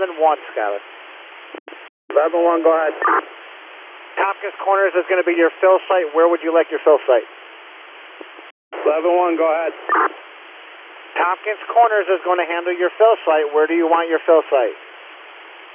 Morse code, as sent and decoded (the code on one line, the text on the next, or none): none